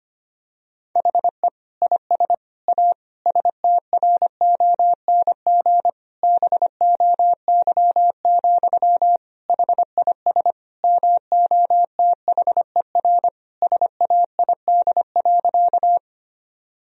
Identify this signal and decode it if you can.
Morse code, heard as a keyed tone, about 25 wpm, 700 Hz. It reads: HE IS A STRONG BOY, HIS MOTHER SAID.
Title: Sentences from Top 1000 Words - Review Part 3 - 25wpm